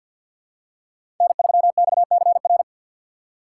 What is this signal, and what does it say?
Morse code, heard as a keyed tone, about 53 wpm, 700 Hz. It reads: N3XCR
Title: USA Call Signs - 5 Characters - 3x Character Spacing - 50wpm